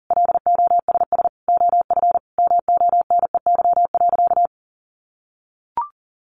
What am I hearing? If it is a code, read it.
Morse code, heard as a keyed tone, about 40 wpm, 700 Hz. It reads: LOSS OF MONEY.
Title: Sentences from Top 1400 Words 40wpm